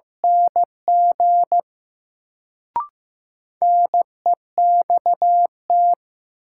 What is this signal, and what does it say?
Morse code, heard as a keyed tone, about 15 wpm, 700 Hz. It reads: NG NEXT
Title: Sets of 2 Words - New Word Drill - 15wpm